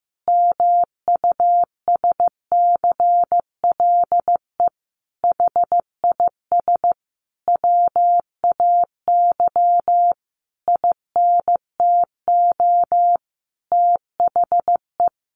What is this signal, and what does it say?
Morse code, heard as a keyed tone, about 15 wpm, 700 Hz. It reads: MUSCLE HIS WAY INTO THE
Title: Sentences from Top 1300 Words 15wpm